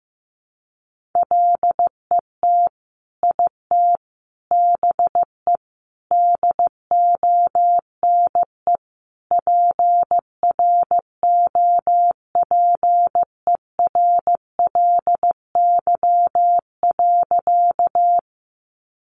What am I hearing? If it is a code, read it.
Morse code, heard as a keyed tone, about 15 wpm, 700 Hz. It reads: LET IT BE DONE PROPERLY.
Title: Sentences from Top 1100 Words 15wpm